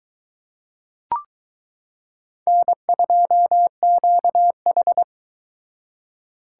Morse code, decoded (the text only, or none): N2QH